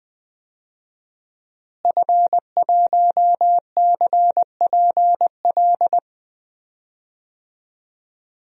Morse code, heard as a keyed tone, about 20 wpm, 700 Hz. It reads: F1CPL
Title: DXCC-100 Call Signs - 5 Characters - Repeated 2x - 20wpm